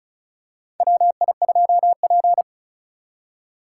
Morse code, heard as a keyed tone, about 36 wpm, 700 Hz. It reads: WI2P
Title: USA Call Signs - 4 Characters - 3x Character Spacing - 35wpm